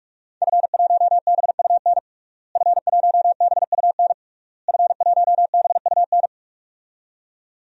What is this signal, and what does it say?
Morse code, heard as a keyed tone, about 46 wpm, 700 Hz. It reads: F1BUN F1BUN F1BUN
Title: DXCC-100 Call Signs - 5 Characters - Repeated 3x - 45wpm